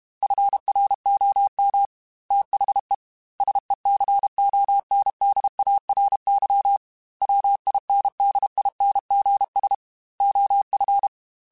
Morse code, heard as a keyed tone, about 32 wpm, 800 Hz. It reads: FROM THE SECONDARY WINDINGS OF
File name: Technical Subjects / Technical Subjects-mp3s / Sw Pwr Sup @30